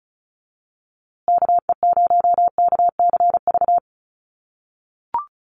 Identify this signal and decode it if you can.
Morse code, heard as a keyed tone, about 34 wpm, 700 Hz. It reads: KE0KCV
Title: USA Call Signs Repeated 3x 35wpm